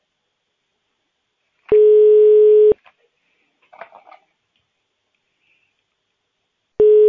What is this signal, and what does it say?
Ringback tone (Europe, 425 Hz).